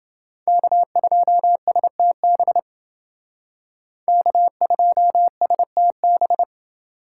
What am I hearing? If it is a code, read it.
Morse code, heard as a keyed tone, about 28 wpm, 700 Hz. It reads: K2STB K2STB